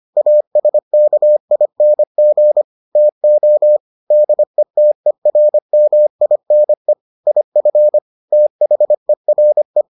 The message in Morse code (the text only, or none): ASKING TO DETERMINE IF THERE